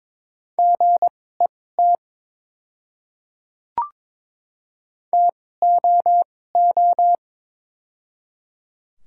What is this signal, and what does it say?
Morse code, heard as a keyed tone, about 22 wpm, spaced as for 16 wpm, 700 Hz. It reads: GET TOO
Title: 2x Character Spacing - Top 300 Words - 22wpm